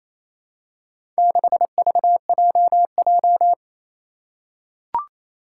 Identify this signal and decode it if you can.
Morse code, heard as a keyed tone, about 28 wpm, 700 Hz. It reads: 6VJJ
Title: Four Challenging Letter-Number Combinations 28wpm